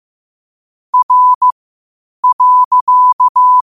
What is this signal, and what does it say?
Morse code, heard as a keyed tone, about 15 wpm, spaced as for 8 wpm, 1000 Hz. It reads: R.
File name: qotdfiles / 8wpm